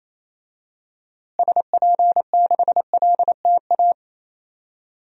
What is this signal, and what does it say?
Morse code, heard as a keyed tone, about 27 wpm, 700 Hz. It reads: SP6LTA
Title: DXCC-100 Call Signs - 6 Characters - Repeated 3x - 28wpm